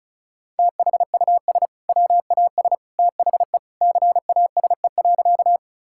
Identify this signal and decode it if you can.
Morse code, heard as a keyed tone, about 36 wpm, 700 Hz. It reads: THUS WAS THE CASE.
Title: Sentences from Top 1000 Words - Review Part 5 - 35wpm